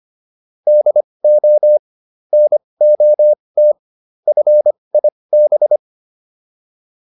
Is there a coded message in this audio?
Yes: Morse code from 0.7 to 5.8 s, heard as a keyed tone, about 25 wpm, spaced as for 20 wpm, 600 Hz. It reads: DO NOT FIB